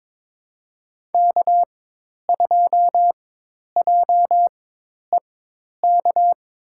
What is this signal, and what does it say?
Morse code, heard as a keyed tone, about 22 wpm, spaced as for 10 wpm, 700 Hz. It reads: K2JEK
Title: USA Call Signs - 4x Character Spacing - 22wpm